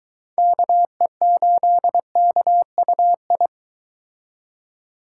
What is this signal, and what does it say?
Morse code, heard as a keyed tone, about 23 wpm, 700 Hz. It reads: KE8KUI